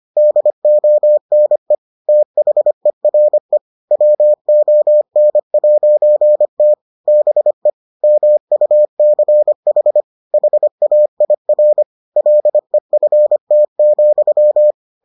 Morse code, heard as a keyed tone, about 25 wpm, 600 Hz. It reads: DONE THERE WON'T BE MUCH HAIR LEFT,